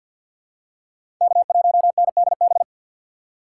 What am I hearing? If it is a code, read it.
Morse code, heard as a keyed tone, about 52 wpm, 700 Hz. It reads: K1NDB
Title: USA Call Signs - 5 Characters - 4x Character Spacing - 50wpm